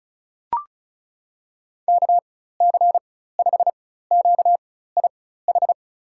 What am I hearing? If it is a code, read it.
Morse code, heard as a keyed tone, about 36 wpm, spaced as for 16 wpm, 700 Hz. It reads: KC5QIH
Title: USA Call Signs - 4x Character Spacing - 35wpm